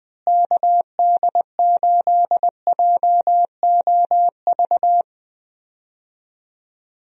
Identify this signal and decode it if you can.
Morse code, heard as a keyed tone, about 21 wpm, 700 Hz. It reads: KD8JOV
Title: USA Call Signs - 6 Characters - Repeated 3x - 20wpm